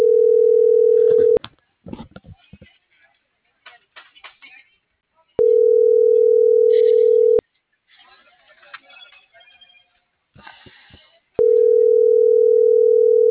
Ringback tone (North America), 440 + 480 Hz.
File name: CENTRAL MULTIMODAL / out rodolfo.fuentes - .65035